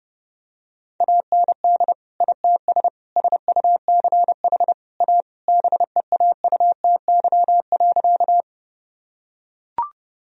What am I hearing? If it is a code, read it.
Morse code, heard as a keyed tone, about 30 wpm, 700 Hz. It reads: AND ITS SUCH A BEAUTY.
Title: Sentences from Top 1000 Words - Review Part 2 - 30wpm